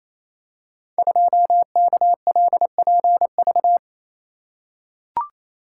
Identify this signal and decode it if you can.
Morse code, heard as a keyed tone, about 29 wpm, 700 Hz. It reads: 2KLPV
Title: Code Groups - 5 Letter:Numbers - 28wpm